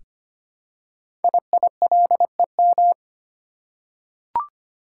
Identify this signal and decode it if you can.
Morse code, heard as a keyed tone, about 25 wpm, 700 Hz. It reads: IILEM
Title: Code Groups - 5 Letter:Numbers - 25wpm